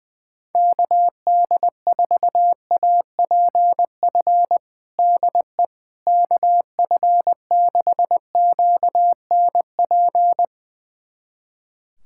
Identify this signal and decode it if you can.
Morse code, heard as a keyed tone, about 20 wpm, 700 Hz. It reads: KD4APF DE KF6QNP
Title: USA Callsign DE Callsign - 6 Characters - 20wpm